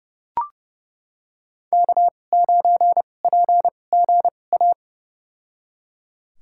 Morse code, heard as a keyed tone, about 30 wpm, spaced as for 22 wpm, 700 Hz. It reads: K9PGA
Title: USA Call Signs - 5 Characters - 2x Character Spacing - 30wpm